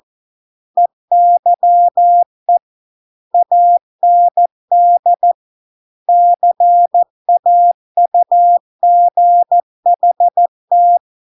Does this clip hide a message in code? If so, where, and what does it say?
Yes: Morse code from 0.8 to 11.0 s, heard as a keyed tone, about 14 wpm, 700 Hz. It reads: EYE AND CAUGHT